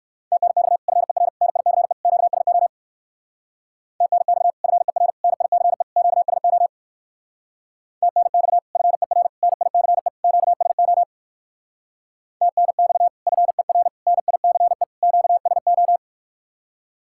Morse code, heard as a keyed tone, about 54 wpm, 700 Hz. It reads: TNX FER NICE QSO TNX FER NICE QSO TNX FER NICE QSO TNX FER NICE QSO
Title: Speed-Racing Common QSO Phrases - 45wpm